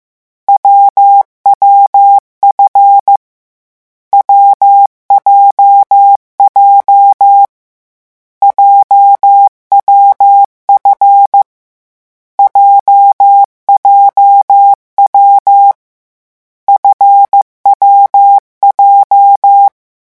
Morse code, heard as a keyed tone, about 15 wpm, 800 Hz. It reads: WWF WJJ JWF JJW FWJ
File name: morse mp3 / lecon7-vitesse 14